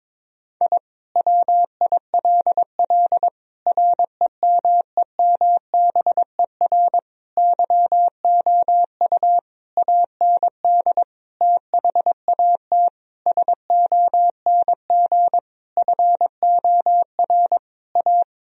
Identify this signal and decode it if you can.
Morse code, heard as a keyed tone, about 22 wpm, 700 Hz. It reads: I WILL REMEMBER YOU AND THAT SONG FOR A
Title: Sentences from Top 1000 Words - Review Part 2 - 22wpm